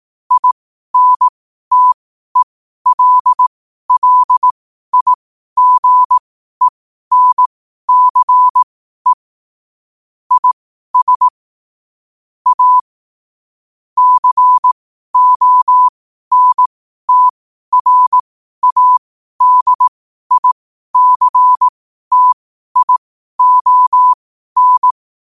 Morse code, heard as a keyed tone, about 18 wpm, spaced as for 12 wpm, 1000 Hz. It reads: INTELLIGENCE IS A CONTRADICTION
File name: qotdfiles / 12wpm